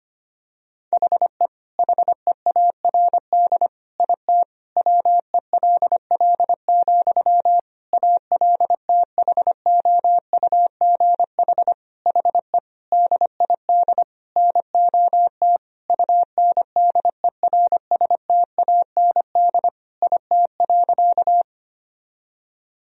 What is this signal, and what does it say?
Morse code, heard as a keyed tone, about 25 wpm, 700 Hz. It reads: HE HEARD IT WELL, ALTHOUGH HE DID NOT UNDERSTAND IT.
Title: Sentences from Top 1000 Words - Review Part 3 - 25wpm